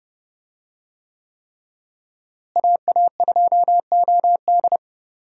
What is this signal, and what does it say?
Morse code, heard as a keyed tone, about 32 wpm, 700 Hz. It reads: AA2OD
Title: USA Call Signs - 5 Characters - Repeated 3x - 30wpm